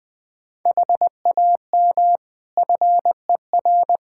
Morse code, heard as a keyed tone, about 20 wpm, 700 Hz. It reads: HAM FER
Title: Practice QSO - Exchange 3 - More Info - 20wpm